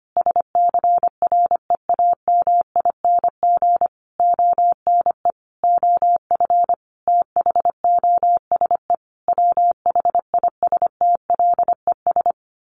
Morse code, heard as a keyed tone, about 25 wpm, 700 Hz. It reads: SCREAMING ONE OF THOSE WHISTLES
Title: A Sound Of Thunder 25wpm